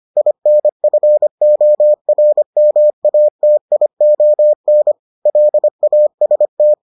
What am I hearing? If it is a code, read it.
Morse code, heard as a keyed tone, about 25 wpm, 600 Hz. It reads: INFORMATION LAST